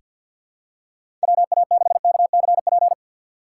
Morse code, heard as a keyed tone, about 46 wpm, 700 Hz. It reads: WA6KCP
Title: USA Call Signs - 6 Characters - 4x Character Spacing - 50wpm